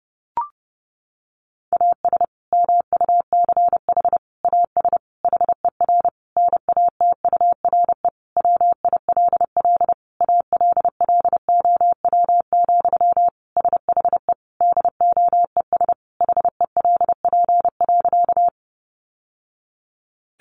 Morse code, heard as a keyed tone, about 30 wpm, 700 Hz. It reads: AS MUCH AS HER NATURE WILL ALLOW, SHE DOES HELP.
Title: Sentences from Top 700 Words 30wpm